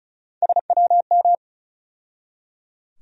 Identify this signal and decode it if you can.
Morse code, heard as a keyed tone, about 36 wpm, 700 Hz. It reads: SWM
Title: USA Call Sign Postfixes - 3 Letters 35wpm